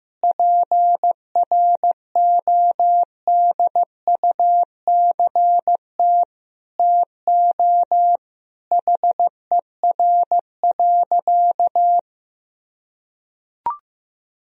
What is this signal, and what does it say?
Morse code, heard as a keyed tone, about 16 wpm, 700 Hz. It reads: PRODUCT TO HER.
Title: Sentences from Top 1000 Words - Review Part 3 - 15wpm